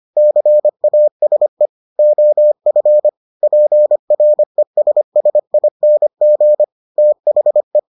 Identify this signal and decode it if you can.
Morse code, heard as a keyed tone, about 25 wpm, 600 Hz. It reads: CASE OF PRESSING THE